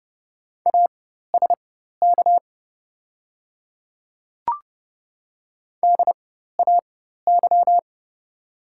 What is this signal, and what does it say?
Morse code, heard as a keyed tone, about 30 wpm, spaced as for 14 wpm, 690 Hz. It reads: ASK DAY
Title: 4x Character Spacing - Top 200 Words - 30wpm